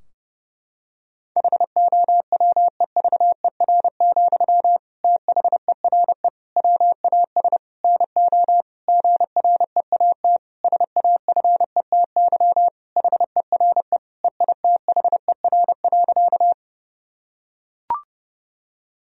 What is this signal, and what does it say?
Morse code, heard as a keyed tone, about 30 wpm, 700 Hz. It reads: HOWEVER, THERE WAS NO GREAT SAFETY HERE EITHER.
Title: Sentences from Top 1000 Words - Review Part 4 - 30wpm